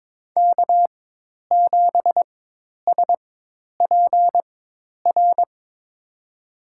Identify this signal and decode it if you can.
Morse code, heard as a keyed tone, about 22 wpm, spaced as for 10 wpm, 700 Hz. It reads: K7SPR
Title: USA Call Signs - 5 Characters - 4x Character Spacing - 22wpm